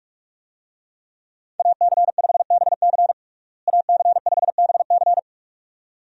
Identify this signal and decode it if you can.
Morse code, heard as a keyed tone, about 44 wpm, 700 Hz. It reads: AC5BC AC5BC
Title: USA Call Signs - 5 Characters - Repeated 2x - 45wpm